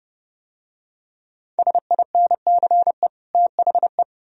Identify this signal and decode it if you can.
Morse code, heard as a keyed tone, about 30 wpm, 700 Hz. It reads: SINCE THE